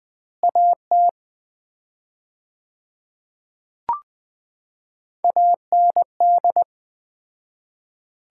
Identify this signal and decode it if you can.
Morse code, heard as a keyed tone, about 20 wpm, 700 Hz. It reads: AT AND
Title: 4x Word-Spacing - Sets of 2 Words - 20wpm